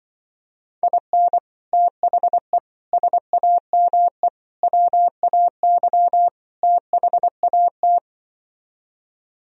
Morse code, heard as a keyed tone, about 25 wpm, 700 Hz. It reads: IN THE SAME WAY THAT